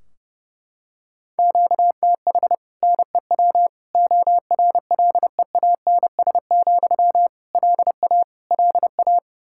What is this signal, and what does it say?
Morse code, heard as a keyed tone, about 32 wpm, 700 Hz. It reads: QTH NEW ORLEANS, LA LA
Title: Practice QSO - Exchange 1 - Beginning - 30wpm